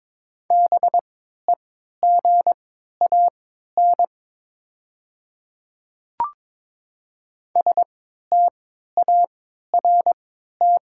Morse code, heard as a keyed tone, about 22 wpm, spaced as for 13 wpm, 700 Hz. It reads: BEGAN START
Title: 3x Character Spacing - Top 500 Words - 22wpm